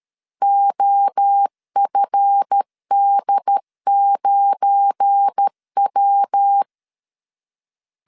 Morse code, heard as a keyed tone, about 13 wpm, 800 Hz. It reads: OFD9W
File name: LEZIONI / 7 GF9Z / 7 GF9 12wpm 2319